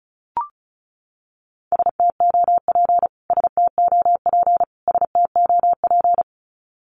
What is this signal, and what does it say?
Morse code, heard as a keyed tone, about 34 wpm, 700 Hz. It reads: STOP STOP STOP
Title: Top 500 Words Repeated 35wpm